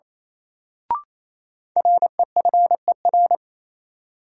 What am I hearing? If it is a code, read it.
Morse code, heard as a keyed tone, about 28 wpm, 700 Hz. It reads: REFER
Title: Sentences from Top 1000 Words - New Word Drill - 28wpm